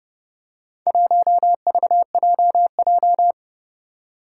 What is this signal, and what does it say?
Morse code, heard as a keyed tone, about 30 wpm, 700 Hz. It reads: 1VJJ
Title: Four Challenging Letter-Number Combinations 30wpm